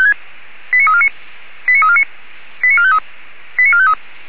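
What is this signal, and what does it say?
Five-tone selective call (ZVEI), heard as quick runs of five tones, tones of about 70 ms. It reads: 89349 89349 89453 89453
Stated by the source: ZVEI five-tone selective call received by SDR.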